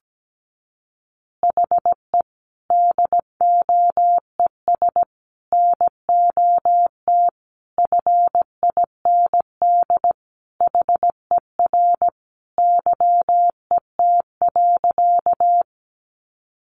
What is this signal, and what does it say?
Morse code, heard as a keyed tone, about 17 wpm, 700 Hz. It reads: HE DOES NOT FIND HER YET.
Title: Easy Phrases 17wpm